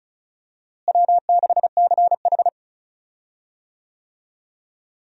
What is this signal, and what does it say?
Morse code, heard as a keyed tone, about 36 wpm, 700 Hz. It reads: W6CH
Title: USA Call Signs - 4 Characters - Repeated 3x - 35wpm